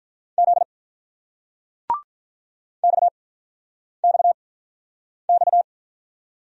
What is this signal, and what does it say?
Morse code, heard as a keyed tone, about 48 wpm, spaced as for 9 wpm, 700 Hz. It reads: Z XXX